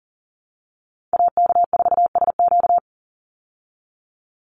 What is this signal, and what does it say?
Morse code, heard as a keyed tone, about 40 wpm, 700 Hz. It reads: AK4SQ
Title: Call Signs Repeated 2x - 40wpm